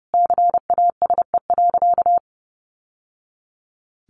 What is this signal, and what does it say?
Morse code, heard as a keyed tone, about 29 wpm, 700 Hz. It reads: CASE.
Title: Sentences from Top 1200 Words 30wpm